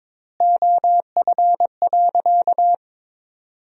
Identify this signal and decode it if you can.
Morse code, heard as a keyed tone, about 22 wpm, 700 Hz. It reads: OF.
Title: 3x Word-Spacing - Sentences from Top 300 Words - 22wpm